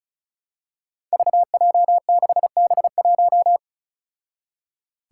Morse code, heard as a keyed tone, about 36 wpm, 700 Hz. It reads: VJ6B1
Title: Five Challenging Letter-Number Combinations 35wpm